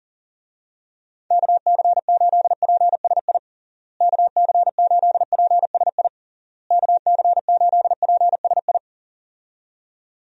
Morse code, heard as a keyed tone, about 40 wpm, 700 Hz. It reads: KC8PSI KC8PSI KC8PSI
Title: USA Call Signs - 6 Characters - Repeated 3x - 40wpm